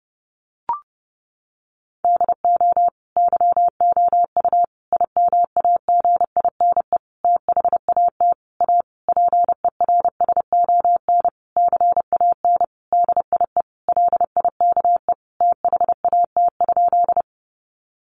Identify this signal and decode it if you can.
Morse code, heard as a keyed tone, about 30 wpm, 700 Hz. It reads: DO YOU IMAGINE THAT A PERSON CAN DIE LIKE THAT?
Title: Sentences from Top 1400 Words 30wpm